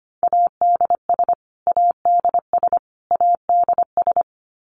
Morse code, heard as a keyed tone, about 25 wpm, 700 Hz. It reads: ADS ADS ADS
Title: Abbreviations Repeated 3x - 25wpm